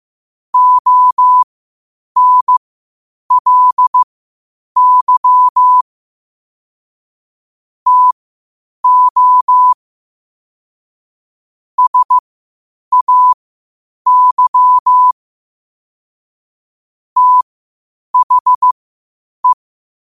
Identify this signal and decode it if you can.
Morse code, heard as a keyed tone, about 15 wpm, spaced as for 8 wpm, 1000 Hz. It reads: ONLY TO SAY THE
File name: qotdfiles / 8wpm